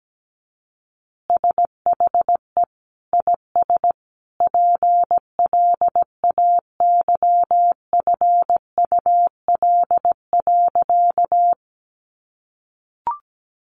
Morse code, heard as a keyed tone, about 17 wpm, 700 Hz. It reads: SHE IS PLAYFUL.
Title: Easy Phrases 17wpm